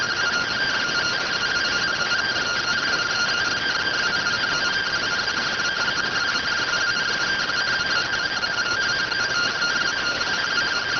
Two-tone teleprinter signal, about 170 Hz shift.